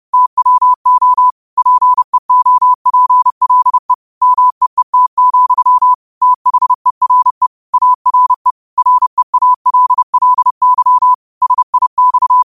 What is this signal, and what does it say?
Morse code, heard as a keyed tone, about 30 wpm, 1000 Hz. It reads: TWO PEOPLE MEET, THERE ARE REALLY SIX